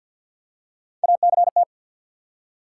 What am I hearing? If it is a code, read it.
Morse code, heard as a keyed tone, about 53 wpm, 700 Hz. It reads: ACT
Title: Three Letter Words - 50wpm